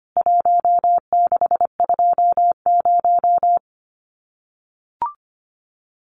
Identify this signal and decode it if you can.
Morse code, heard as a keyed tone, about 25 wpm, 700 Hz. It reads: 1620
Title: CWT Exchange 25wpm